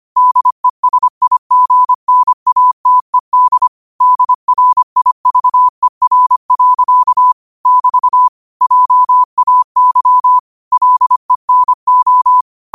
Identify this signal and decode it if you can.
Morse code, heard as a keyed tone, about 25 wpm, 1000 Hz. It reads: DESIGNATED DRIVER. <BT> JAY LENO
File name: qotdfiles / 25wpm